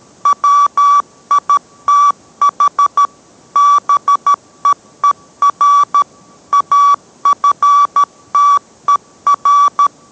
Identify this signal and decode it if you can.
Morse code, heard as a keyed tone, about 13 wpm, 1150 Hz. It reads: WITH BEER AFTER